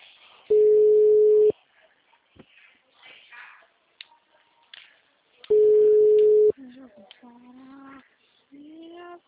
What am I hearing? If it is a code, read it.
Ringback tone (Europe, 425 Hz), with speech.